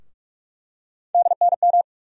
Morse code, heard as a keyed tone, about 46 wpm, 700 Hz. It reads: DNM
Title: USA Call Sign Postfixes - 3 Letters 45wpm